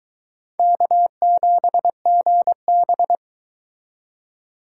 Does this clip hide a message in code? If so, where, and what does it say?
Yes: Morse code from 0.6 to 3.1 s, heard as a keyed tone, about 22 wpm, 700 Hz. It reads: K7GB